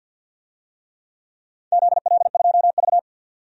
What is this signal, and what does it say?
Morse code, heard as a keyed tone, about 51 wpm, 700 Hz. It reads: ZL2V
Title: DXCC-100 Call Signs - 4 Characters - 50wpm